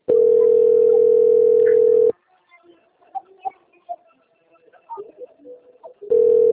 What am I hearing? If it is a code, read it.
Ringback tone (North America), 440 + 480 Hz.